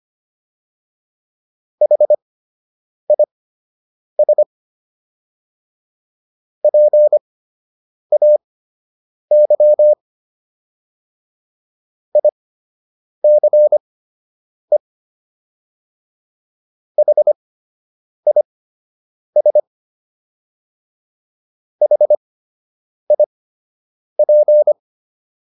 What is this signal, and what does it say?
Morse code, heard as a keyed tone, about 25 wpm, spaced as for 8 wpm, 600 Hz. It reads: HIS PAY ICE HIS HIP